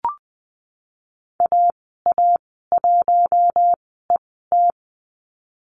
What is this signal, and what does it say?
Morse code, heard as a keyed tone, about 20 wpm, spaced as for 15 wpm, 700 Hz. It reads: AA1ET
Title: 069 Lesson 12g - Letter d - Call Signs 20wpm